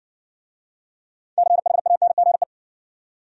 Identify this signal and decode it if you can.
Morse code, heard as a keyed tone, about 60 wpm, 700 Hz. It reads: CHANGE